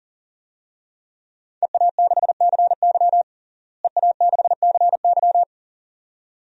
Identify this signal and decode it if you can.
Morse code, heard as a keyed tone, about 37 wpm, 700 Hz. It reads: EA6CY EA6CY
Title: DXCC-100 Call Signs - 5 Characters - Repeated 2x - 40wpm